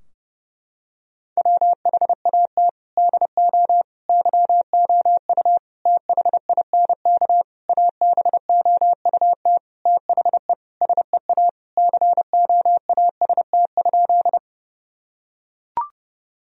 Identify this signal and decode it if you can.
Morse code, heard as a keyed tone, about 30 wpm, 700 Hz. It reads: WHAT DO YOU THINK ABOUT THE SEA COAST?
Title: Sentences from Top 1000 Words - Review Part 4 - 30wpm